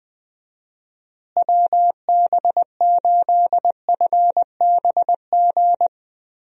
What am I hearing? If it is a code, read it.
Morse code, heard as a keyed tone, about 21 wpm, 700 Hz. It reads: WB8FBG
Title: USA Call Signs - 6 Characters - Repeated 2x - 20wpm